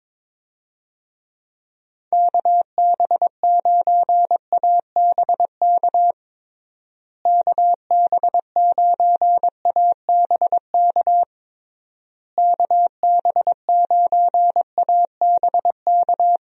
Morse code, heard as a keyed tone, about 22 wpm, 700 Hz. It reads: KB9ABK KB9ABK KB9ABK
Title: USA Call Signs - 6 Characters - Repeated 3x - 22wpm